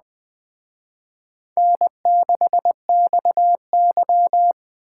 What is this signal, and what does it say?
Morse code, heard as a keyed tone, about 23 wpm, 700 Hz. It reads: N6XY
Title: Speed Racing - USA Call Signs 20wpm